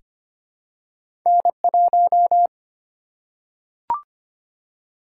Morse code, heard as a keyed tone, about 25 wpm, 700 Hz. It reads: N1
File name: Two Letter-Number Combinations - 25wpm